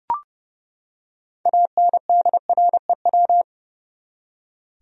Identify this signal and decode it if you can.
Morse code, heard as a keyed tone, about 32 wpm, 700 Hz. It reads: ANDREW
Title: USA Common Names - 30wpm